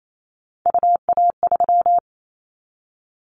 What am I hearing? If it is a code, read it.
Morse code, heard as a keyed tone, about 28 wpm, 700 Hz. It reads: UA3
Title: Callsign Prefixes for Top 50 DXCC - 28wpm